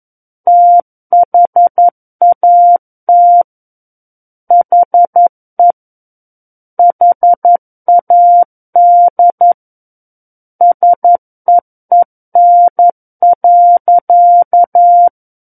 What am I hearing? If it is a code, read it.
Morse code, heard as a keyed tone, about 11 wpm, 700 Hz. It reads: THAT HE HAD SEEN.
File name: 11 WPM